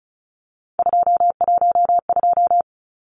Morse code, heard as a keyed tone, about 34 wpm, 700 Hz. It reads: 212
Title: Numbers - 3-Digit 35wpm